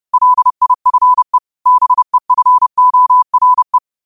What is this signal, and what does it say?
Morse code, heard as a keyed tone, about 30 wpm, 1000 Hz. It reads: LIFE BEFORE
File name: qotdfiles / 30wpm